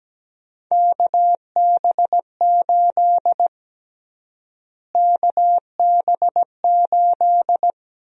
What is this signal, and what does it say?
Morse code, heard as a keyed tone, about 17 wpm, 700 Hz. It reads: KB8 KB8
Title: USA Call Sign Prefixes - Repeated 2x - 17wpm